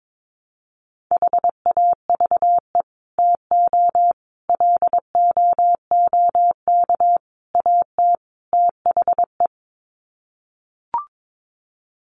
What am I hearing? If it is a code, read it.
Morse code, heard as a keyed tone, about 22 wpm, 700 Hz. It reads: HAVE TO LOOK AT THE
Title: Sets of 5 Words Encore 22wpm